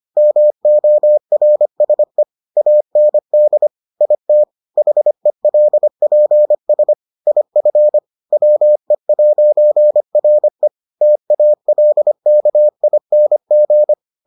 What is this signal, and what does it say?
Morse code, heard as a keyed tone, about 25 wpm, 600 Hz. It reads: MORSE AND IT HELPS IF WE'RE TALKING